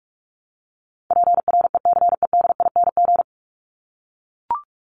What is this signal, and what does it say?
Morse code, heard as a keyed tone, about 44 wpm, 700 Hz. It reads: PRECEDING
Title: ING Suffix - Top 100 Words with Base Words 45wpm